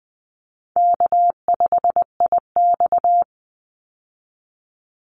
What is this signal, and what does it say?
Morse code, heard as a keyed tone, about 20 wpm, 700 Hz. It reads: K5IX
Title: Call Signs Repeated 2x - 20wpm